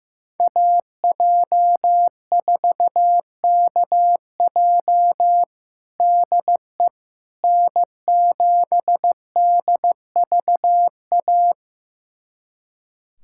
Morse code, heard as a keyed tone, about 15 wpm, 700 Hz. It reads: AJ4KJ DE N7DVA
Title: USA Callsign DE Callsign - 5 Characters - 15wpm